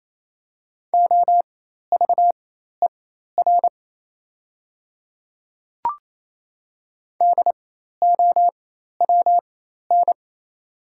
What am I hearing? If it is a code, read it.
Morse code, heard as a keyed tone, about 27 wpm, spaced as for 13 wpm, 700 Hz. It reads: OVER DOWN
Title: 4x Character Spacing - Top 300 Words - 28wpm